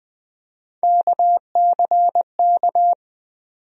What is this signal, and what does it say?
Morse code, heard as a keyed tone, about 20 wpm, 700 Hz. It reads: KCK
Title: USA Call Sign Postfixes - 3 Letters 20wpm